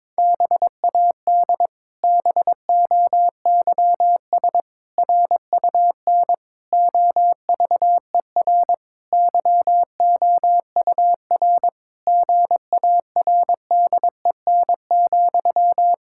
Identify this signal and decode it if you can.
Morse code, heard as a keyed tone, about 22 wpm, 700 Hz. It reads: BAD BOYS RUN OVER YOUR GARDEN,
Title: Pangram Challenge v2 - 22wpm